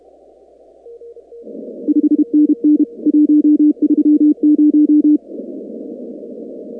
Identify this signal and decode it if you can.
Morse code, heard as a keyed tone, about 32 wpm, 300 Hz. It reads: 5NN 130
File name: contests / 2019-cw-wpx-kt5j-k5pi- -2330 0-19